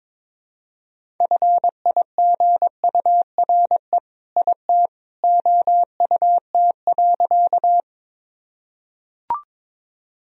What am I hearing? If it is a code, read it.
Morse code, heard as a keyed tone, about 22 wpm, 700 Hz. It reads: FIGURE IT OUT.
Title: Sentences from Top 1000 Words - Review Part 2 - 22wpm